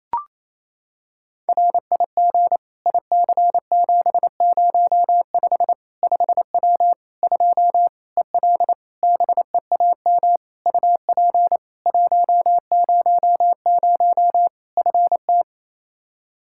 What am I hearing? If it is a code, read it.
Morse code, heard as a keyed tone, about 28 wpm, 700 Hz. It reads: RIG IC705 5W 2 EL BEAM UP 100 FT
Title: K1USN SST QSO Practice 28wpm